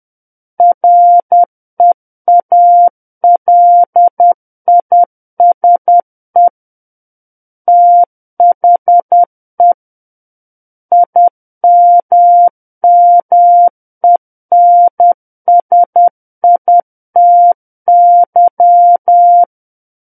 Morse code, heard as a keyed tone, about 10 wpm, 700 Hz. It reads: REALISE THE IMMENSITY
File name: Book One / 01-Chapter 1 - 10 WPM